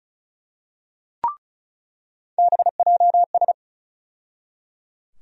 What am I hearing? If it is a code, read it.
Morse code, heard as a keyed tone, about 34 wpm, 700 Hz. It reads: BJS
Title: USA Call Sign Postfixes - 3 Letters 35wpm